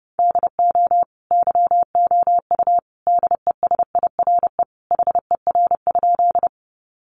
Morse code, heard as a keyed tone, about 30 wpm, 700 Hz. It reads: DO YOU DESIRE HER?
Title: Sentences from Top 1100 Words 30wpm